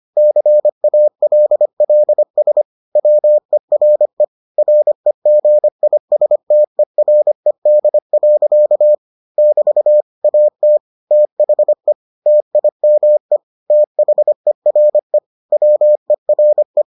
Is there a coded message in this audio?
Yes: Morse code from 0.2 to 16.8 s, heard as a keyed tone, about 25 wpm, 600 Hz. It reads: CALLS WERE REGISTERED. <BT> AT THE TIME THERE WERE